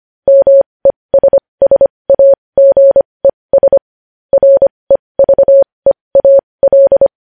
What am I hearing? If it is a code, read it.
Morse code, heard as a keyed tone, about 25 wpm, spaced as for 20 wpm, 550 Hz. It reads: MESSAGES REVEAL
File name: podcast / News headlines-1x-25x20